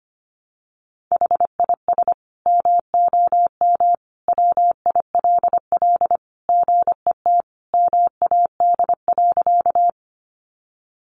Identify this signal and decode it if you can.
Morse code, heard as a keyed tone, about 25 wpm, 700 Hz. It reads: HIS MOM WILL GET MAD.
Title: Additional Easy Phrases 25wpm